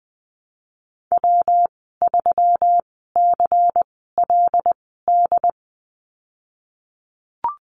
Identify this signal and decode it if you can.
Morse code, heard as a keyed tone, about 20 wpm, spaced as for 14 wpm, 700 Hz. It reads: W3CLD
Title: 122 Lesson 20g - Number 3 - Call Signs 20wpm